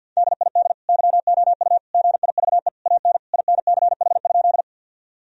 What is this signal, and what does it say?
Morse code, heard as a keyed tone, about 50 wpm, 700 Hz. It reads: DID YOU GIVE AN INCH?
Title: Sentences from Top 1000 Words - Review Part 3 - 50wpm